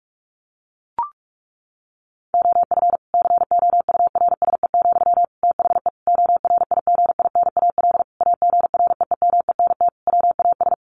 Morse code, heard as a keyed tone, about 44 wpm, 700 Hz. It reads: OF COURSE, THE ORIGINAL AGREEMENT WAS
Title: Sentences from Top 1500 Words 45wpm